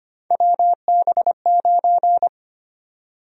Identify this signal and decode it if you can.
Morse code, heard as a keyed tone, about 25 wpm, 700 Hz. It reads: WB9
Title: USA Call Sign Prefixes - Repeated 2x - 25wpm